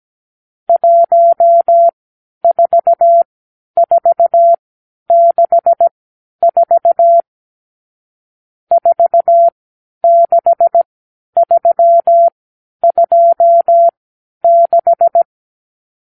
Morse code, heard as a keyed tone, about 17 wpm, spaced as for 10 wpm, 680 Hz. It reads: 14464 46326
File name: k7qo / Lesson 50 (1-6)